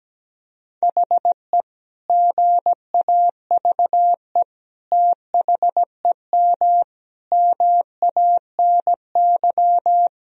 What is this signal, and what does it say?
Morse code, heard as a keyed tone, about 17 wpm, 700 Hz. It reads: HE GAVE THEM MANY
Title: Sentences from Top 1000 Words - Review Part 3 - 17wpm